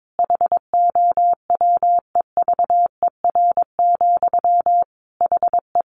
Morse code, heard as a keyed tone, about 22 wpm, 700 Hz. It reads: HOWEVER, HE
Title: Sentences from Top 600 Words 22wpm